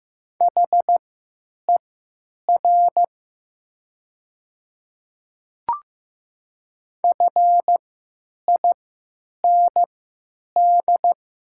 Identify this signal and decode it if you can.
Morse code, heard as a keyed tone, about 15 wpm, spaced as for 9 wpm, 700 Hz. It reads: HER FIND
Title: 3x Character Spacing - Top 100 Words - 15wpm